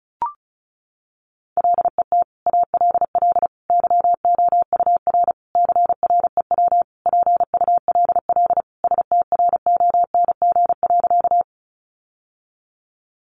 Morse code, heard as a keyed tone, about 34 wpm, 700 Hz. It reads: LET ALL YOUR CREW PULL STRONG.
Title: Sentences from Top 1500 Words 35wpm